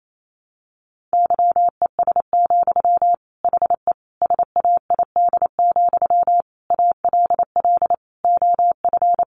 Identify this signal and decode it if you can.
Morse code, heard as a keyed tone, about 28 wpm, 700 Hz. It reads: YES, HE SAID, ALL OF
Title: Sentences from Top 500 Words 28wpm